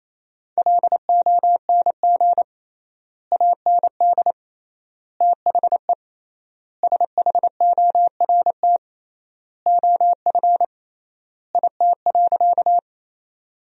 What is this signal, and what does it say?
Morse code, heard as a keyed tone, about 28 wpm, 700 Hz. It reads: LONG AND THE SHORT OF IT.
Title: 3x Word-Spacing - Sentences from Top 400 Words - 28wpm